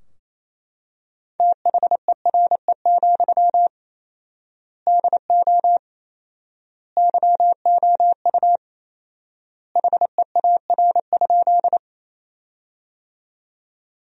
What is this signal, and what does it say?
Morse code, heard as a keyed tone, about 28 wpm, 700 Hz. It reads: THERE, DO YOU HEAR?
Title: 4x Word-Spacing - Sentences from Top 300 Words - 28wpm